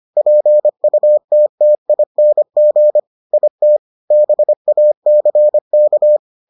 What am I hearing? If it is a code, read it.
Morse code, heard as a keyed tone, about 25 wpm, 600 Hz. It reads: PUTTING IT BACK